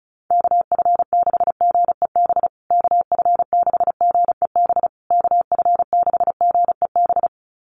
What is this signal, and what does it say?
Morse code, heard as a keyed tone, about 34 wpm, 700 Hz. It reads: KF6GEB KF6GEB KF6GEB
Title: USA Call Signs Repeated 3x 35wpm